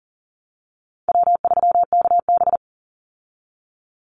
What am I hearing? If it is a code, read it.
Morse code, heard as a keyed tone, about 40 wpm, 700 Hz. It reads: W3KB
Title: Easy Call Signs Repeated 3x - 40wpm